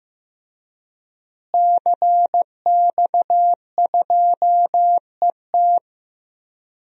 Morse code, heard as a keyed tone, about 15 wpm, 700 Hz. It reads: CX2ET
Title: DXCC-100 Call Signs - 5 Characters - Repeated 3x - 15wpm